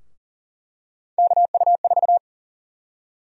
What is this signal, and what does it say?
Morse code, heard as a keyed tone, about 38 wpm, 700 Hz. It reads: KU4
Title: USA Call Sign Prefixes - 40wpm